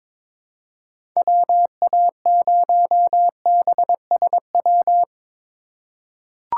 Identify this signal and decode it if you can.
Morse code, heard as a keyed tone, about 22 wpm, 700 Hz. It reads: WA0BSW
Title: USA Call Signs - 6 Characters - 4x Character Spacing - 22wpm